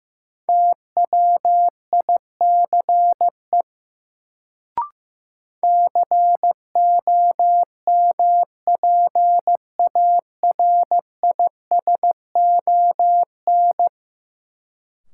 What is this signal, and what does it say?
Morse code, heard as a keyed tone, about 15 wpm, 700 Hz. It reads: TWICE COMPARISON
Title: Sentences from Top 1600 Words - New Word Drill - 15wpm